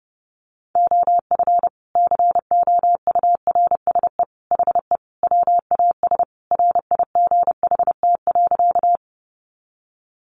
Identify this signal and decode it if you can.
Morse code, heard as a keyed tone, about 28 wpm, 700 Hz. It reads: OF COURSE HE WAS RIGHT.
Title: Sentences from Top 500 Words 30wpm